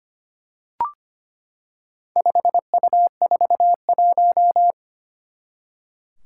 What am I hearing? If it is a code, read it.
Morse code, heard as a keyed tone, about 25 wpm, 700 Hz. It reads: 5U41
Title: Four Challenging Letter-Number Combinations 25wpm